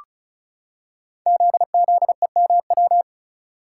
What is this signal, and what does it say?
Morse code, heard as a keyed tone, about 36 wpm, 690 Hz. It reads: ZZEMW
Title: Code Groups - 5 Letters - 35wpm